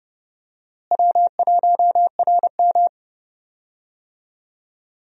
Morse code, heard as a keyed tone, about 32 wpm, 700 Hz. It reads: W1RM
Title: USA Call Signs - 4 Characters - Repeated 3x - 30wpm